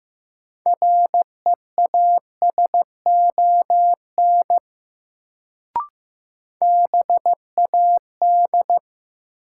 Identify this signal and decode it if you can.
Morse code, heard as a keyed tone, about 15 wpm, 700 Hz. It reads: REASON BAD
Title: Sentences from Top 700 Words - New Word Drill - 15wpm